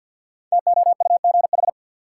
Morse code, heard as a keyed tone, about 49 wpm, 700 Hz. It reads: TOUGH
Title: Sentences from Top 1200 Words - New Word Drill - 50wpm